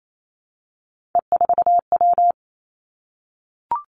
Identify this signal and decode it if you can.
Morse code, heard as a keyed tone, about 28 wpm, 700 Hz. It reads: E4W
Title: Three Challenging Letter-Number Combinations 28wpm